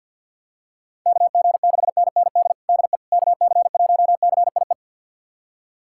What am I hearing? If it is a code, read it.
Morse code, heard as a keyed tone, about 51 wpm, 700 Hz. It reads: KG6NND DE KC1CIE
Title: USA Callsign DE Callsign - 6 Characters - 50wpm